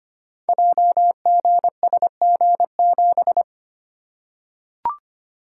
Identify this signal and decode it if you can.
Morse code, heard as a keyed tone, about 25 wpm, 700 Hz. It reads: JGSG7
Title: Code Groups - 5 Letter:Numbers - 25wpm